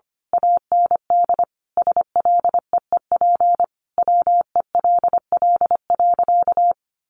Morse code, heard as a keyed tone, about 25 wpm, 700 Hz. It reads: AND SLEEP WELL.
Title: Sentences from Top 500 Words 25wpm